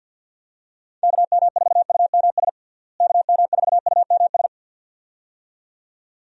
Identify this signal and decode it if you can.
Morse code, heard as a keyed tone, about 51 wpm, 700 Hz. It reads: KM4UMS KM4UMS
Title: USA Call Signs - 6 Characters - Repeated 2x - 50wpm